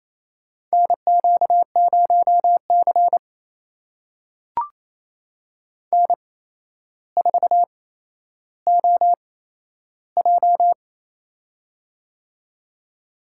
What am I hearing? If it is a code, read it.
Morse code, heard as a keyed tone, about 28 wpm, spaced as for 12 wpm, 700 Hz. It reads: NQ0C N4OJ
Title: USA Call Signs - 4 Characters - 8x Character Spacing - 28wpm